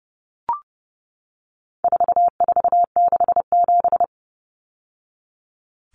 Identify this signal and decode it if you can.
Morse code, heard as a keyed tone, about 30 wpm, 700 Hz. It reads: 4467
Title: Numbers - 4-Digit 30wpm